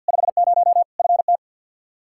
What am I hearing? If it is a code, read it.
Morse code, heard as a keyed tone, about 53 wpm, 700 Hz. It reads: 50 FT
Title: Practice QSO - Exchange 2 - Info - 50wpm